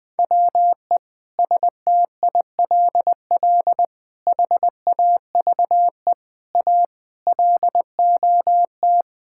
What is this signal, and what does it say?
Morse code, heard as a keyed tone, about 20 wpm, 700 Hz. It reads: WE STILL HAVE A LOT
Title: Sentences with 8-Word Phrases - 20wpm